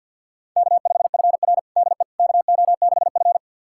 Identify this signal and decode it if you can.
Morse code, heard as a keyed tone, about 47 wpm, 700 Hz. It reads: K5FR DE KO6F
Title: USA Callsign DE Callsign - 4 Characters - 50wpm